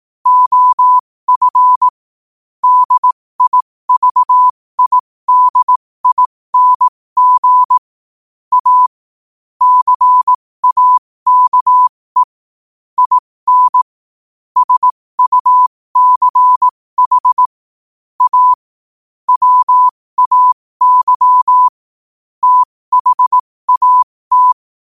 Morse code, heard as a keyed tone, about 18 wpm, spaced as for 15 wpm, 1000 Hz. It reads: OF DIVIDING A CAKE IN SUCH A WAY THAT